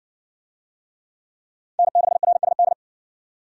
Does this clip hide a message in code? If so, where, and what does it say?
Yes: Morse code from 1.8 to 2.7 s, heard as a keyed tone, about 60 wpm, 700 Hz. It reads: N6RSD